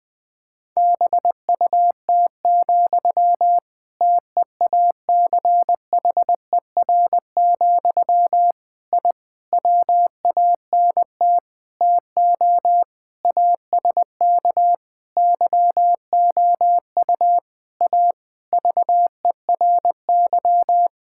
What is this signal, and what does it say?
Morse code, heard as a keyed tone, about 20 wpm, 700 Hz. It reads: BUT, TEACHER, I WANT TO ASK YOU A VERY
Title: Sentences from Top 1000 Words - Review Part 6 - 20wpm